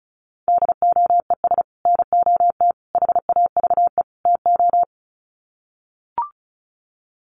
Morse code, heard as a keyed tone, about 34 wpm, 700 Hz. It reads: DOES NOT HAVE TO
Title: Sets of 4 Words 35wpm